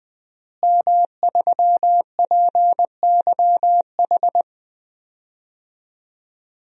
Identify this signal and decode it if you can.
Morse code, heard as a keyed tone, about 20 wpm, 700 Hz. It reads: M3PYH
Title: DXCC-100 Call Signs - 5 Characters - Repeated 3x - 20wpm